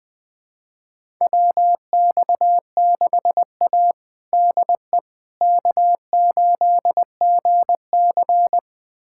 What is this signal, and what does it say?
Morse code, heard as a keyed tone, about 21 wpm, 700 Hz. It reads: WX6A DE K8GC
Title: USA Callsign DE Callsign - 4 Characters - 20wpm